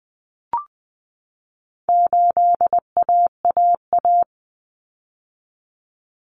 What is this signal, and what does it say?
Morse code, heard as a keyed tone, about 20 wpm, 700 Hz. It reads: 8AAA
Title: Common Cut Numbers - 4-Digit 20wpm